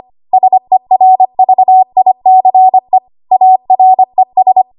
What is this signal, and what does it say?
Morse code, heard as a keyed tone, about 25 wpm, 750 Hz. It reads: SERVICE ARES
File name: Archive / 25 WPM / 25WPM